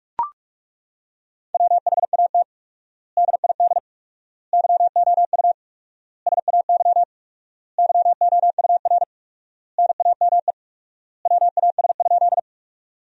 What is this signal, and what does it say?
Morse code, heard as a keyed tone, about 41 wpm, 700 Hz. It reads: WHAT DID YOU SAY YOUR NAME WAS?
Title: 4x Word-Spacing - Sentences from Top 200 Words - 45wpm